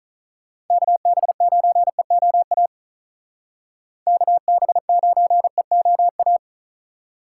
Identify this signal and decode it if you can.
Morse code, heard as a keyed tone, about 46 wpm, 700 Hz. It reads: KB9EOA KB9EOA